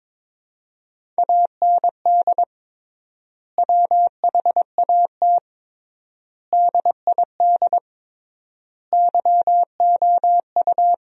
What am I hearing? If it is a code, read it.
Morse code, heard as a keyed tone, about 22 wpm, 700 Hz. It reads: AND WHAT DID YOU
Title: 3x Word-Spacing - Sentences from Top 300 Words - 22wpm